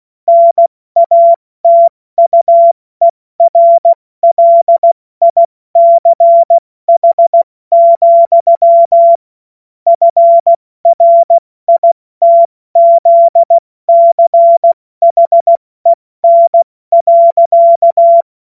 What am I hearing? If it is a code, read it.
Morse code, heard as a keyed tone, about 16 wpm, 680 Hz. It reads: NATUERLICH, FRITZCHEN.
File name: Morsekurs / Witze / Witz 21. September 2021, 70 F80